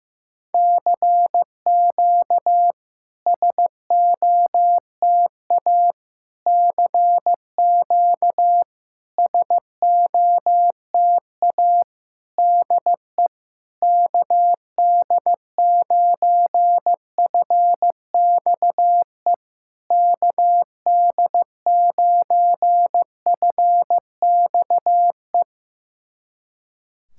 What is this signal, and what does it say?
Morse code, heard as a keyed tone, about 15 wpm, 700 Hz. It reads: CQ SOTA CQ SOTA DE KD9FXE KD9FXE
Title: SOTA QSO Practice 15wpm